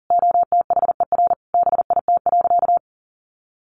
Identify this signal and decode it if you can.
Morse code, heard as a keyed tone, about 40 wpm, 700 Hz. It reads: OTHER BIT.
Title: Sentences from Top 900 Words 40wpm